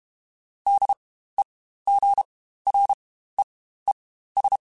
Morse code, heard as a keyed tone, about 32 wpm, spaced as for 15 wpm, 800 Hz. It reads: DEGREES
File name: Spaced Characters / Spaced Characters-mp3s / Weather 4 14 sp @30